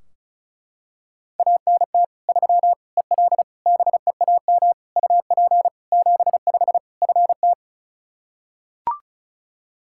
Morse code, heard as a keyed tone, about 36 wpm, 700 Hz. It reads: ANT 3 EL BEAM UP 75 FT
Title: Practice QSO - Exchange 2 - Info - 35wpm